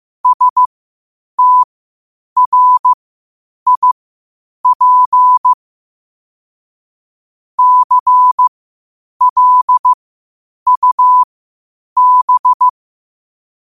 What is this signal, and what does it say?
Morse code, heard as a keyed tone, about 15 wpm, spaced as for 8 wpm, 1000 Hz. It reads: STRIP CLUB
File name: qotdfiles / 8wpm